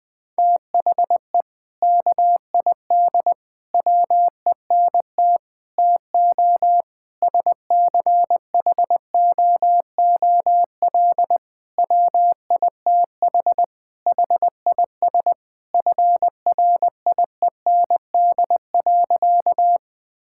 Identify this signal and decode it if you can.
Morse code, heard as a keyed tone, about 20 wpm, 700 Hz. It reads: THE KID WENT TO SCHOOL WITH HIS FRIEND.
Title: Sentences from Top 1600 Words 20wpm